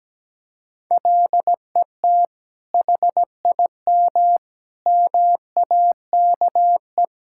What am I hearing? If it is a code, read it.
Morse code, heard as a keyed tone, about 17 wpm, 700 Hz. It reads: LET HIM MAKE
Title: Sentences from Top 1000 Words - Review Part 6 - 17wpm